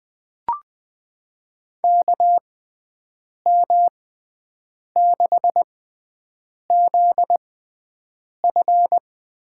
Morse code, heard as a keyed tone, about 20 wpm, spaced as for 7 wpm, 700 Hz. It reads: KM6ZF
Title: USA Call Signs - 6 Characters - 6x Character Spacing - 20wpm